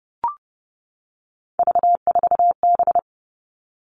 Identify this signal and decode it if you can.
Morse code, heard as a keyed tone, about 28 wpm, 700 Hz. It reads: V4B
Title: All Cut Numbers - 3-Digit 30wpm